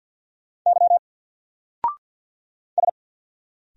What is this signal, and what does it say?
Morse code, heard as a keyed tone, about 50 wpm, 700 Hz. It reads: Y S
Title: Single Letter-Number 50wpm